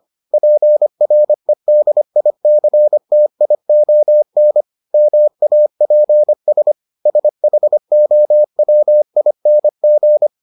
Morse code, heard as a keyed tone, about 25 wpm, 600 Hz. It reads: PREDICTION MAPS SHOWING